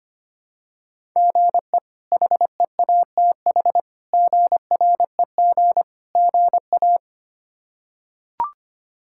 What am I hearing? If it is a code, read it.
Morse code, heard as a keyed tone, about 25 wpm, 700 Hz. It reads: GE HEATH GREG GA
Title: K1USN SST QSO Practice - 25wpm